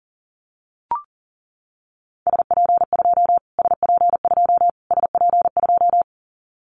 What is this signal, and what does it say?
Morse code, heard as a keyed tone, about 40 wpm, 700 Hz. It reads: SP2 SP2 SP2
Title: Callsign Prefixes for Top 25 DXCC - Repeated 3x - 40wpm